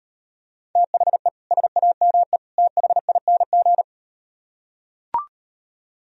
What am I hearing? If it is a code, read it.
Morse code, heard as a keyed tone, about 38 wpm, 700 Hz. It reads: THE SAME THING